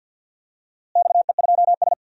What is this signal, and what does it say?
Morse code, heard as a keyed tone, about 52 wpm, 700 Hz. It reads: XE2S
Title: DXCC-100 Call Signs - 4 Characters - Repeated 2x - 50wpm